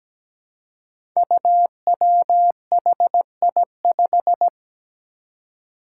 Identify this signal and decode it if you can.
Morse code, heard as a keyed tone, about 17 wpm, 700 Hz. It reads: UWHI5
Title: Five Challenging Letter-Number Combinations 17wpm